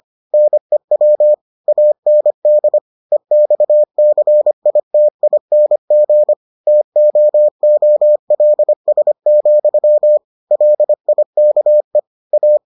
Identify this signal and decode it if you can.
Morse code, heard as a keyed tone, about 25 wpm, 600 Hz. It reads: NEW AND EXCITING TOOLS, LIKE A